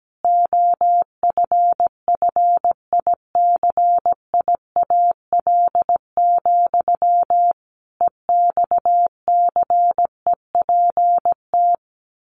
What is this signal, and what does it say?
Morse code, heard as a keyed tone, about 17 wpm, 700 Hz. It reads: OFFICIAL, EXCEPT
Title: A Sound Of Thunder 17wpm